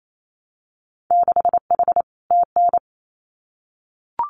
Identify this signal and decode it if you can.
Morse code, heard as a keyed tone, about 28 wpm, 700 Hz. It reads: 6H TN
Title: Winter Field Day Exchanges Encore 28wpm